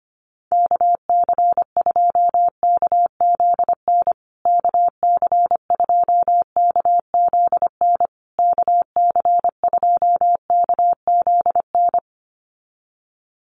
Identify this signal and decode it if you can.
Morse code, heard as a keyed tone, about 25 wpm, 700 Hz. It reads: KC2KZN KC2KZN KC2KZN
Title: USA Call Signs Repeated 3x 25wpm